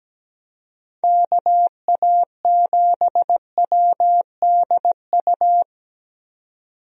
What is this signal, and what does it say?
Morse code, heard as a keyed tone, about 17 wpm, 700 Hz. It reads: KA7WDU
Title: USA Call Signs - 6 Characters - 17wpm